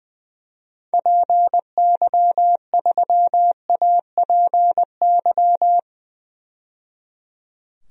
Morse code, heard as a keyed tone, about 20 wpm, 700 Hz. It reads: PY3APY
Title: DXCC-100 Call Signs - 6 Characters - 20wpm